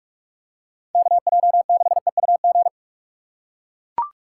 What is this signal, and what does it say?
Morse code, heard as a keyed tone, about 42 wpm, 700 Hz. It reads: KJ6EUG